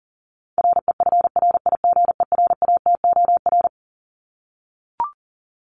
Morse code, heard as a keyed tone, about 40 wpm, 700 Hz. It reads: REFRIGERATOR
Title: Top 1501-1750 Words 40wpm